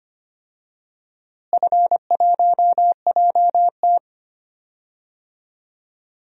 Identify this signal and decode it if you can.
Morse code, heard as a keyed tone, about 25 wpm, 700 Hz. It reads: F1JT
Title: DXCC-100 Call Signs - 4 Characters - Repeated 3x - 25wpm